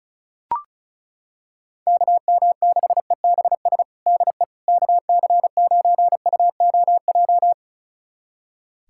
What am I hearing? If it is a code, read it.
Morse code, heard as a keyed tone, about 36 wpm, 700 Hz. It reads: KM6EBS DE KC9UOJ
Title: USA Callsign DE Callsign - 6 Characters - 35wpm